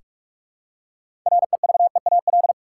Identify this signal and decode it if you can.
Morse code, heard as a keyed tone, about 47 wpm, 700 Hz. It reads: REVEAL
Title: Six Letter Words - 45wpm